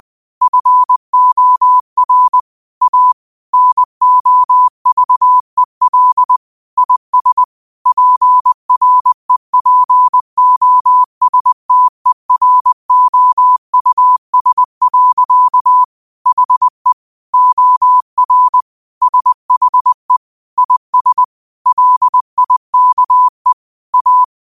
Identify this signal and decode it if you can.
Morse code, heard as a keyed tone, about 20 wpm, 1000 Hz. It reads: FOR A NOVEL IS PREPOSTEROUS. HE OR SHE IS LIKE A
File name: qotdfiles / 20wpm